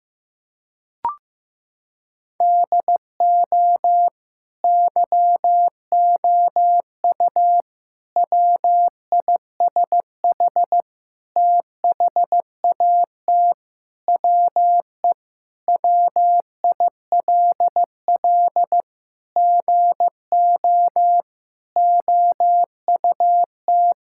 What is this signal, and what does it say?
Morse code, heard as a keyed tone, about 15 wpm, 700 Hz. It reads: DO YOU WISH THAT WE WILL GO OUT
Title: Sentences from Top 1600 Words 15wpm